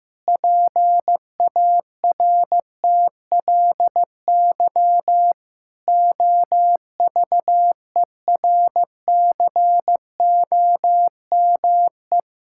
Morse code, heard as a keyed tone, about 15 wpm, 700 Hz. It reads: PARTLY OVERCOME
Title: Pangram Challenge v2 - 15wpm